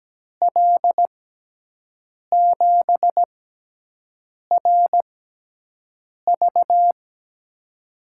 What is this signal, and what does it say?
Morse code, heard as a keyed tone, about 17 wpm, spaced as for 6 wpm, 700 Hz. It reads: L7RV
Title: USA Call Signs - 5 Characters - 6x Character Spacing - 17wpm